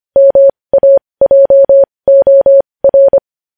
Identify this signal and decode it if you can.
Morse code, heard as a keyed tone, about 25 wpm, spaced as for 20 wpm, 550 Hz. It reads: MAJOR
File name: podcast / News headlines-1x-25x20